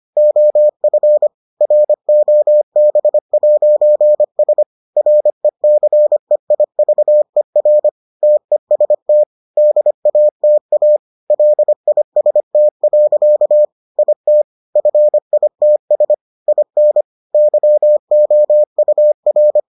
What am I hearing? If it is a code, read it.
Morse code, heard as a keyed tone, about 25 wpm, 600 Hz. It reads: OF ROB'S RECEIVER TEST DATA LIST. IT FITS IN YOUR